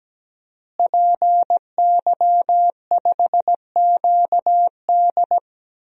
Morse code, heard as a keyed tone, about 17 wpm, 700 Hz. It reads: PY5QD
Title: DXCC-100 Call Signs - 5 Characters - Repeated 3x - 17wpm